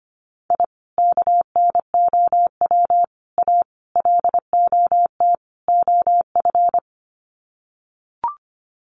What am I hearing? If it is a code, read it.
Morse code, heard as a keyed tone, about 25 wpm, 700 Hz. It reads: I KNOW A LOT OF
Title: Sets of 5 Words Encore 25wpm